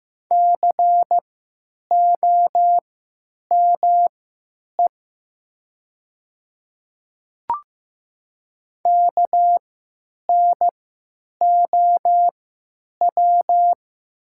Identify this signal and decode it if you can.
Morse code, heard as a keyed tone, about 15 wpm, spaced as for 8 wpm, 700 Hz. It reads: COME KNOW
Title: 3x Character Spacing - Top 100 Words - 15wpm